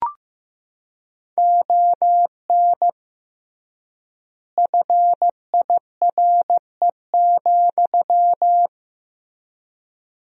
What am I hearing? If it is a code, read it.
Morse code, heard as a keyed tone, about 15 wpm, 700 Hz. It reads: ON FIRE,
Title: 3x Word-Spacing - Sentences from Top 400 Words - 15wpm